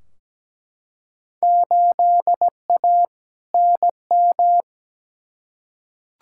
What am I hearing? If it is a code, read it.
Morse code, heard as a keyed tone, about 17 wpm, 700 Hz. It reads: 8A NM
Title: Field Day Exchanges Encore - 17wpm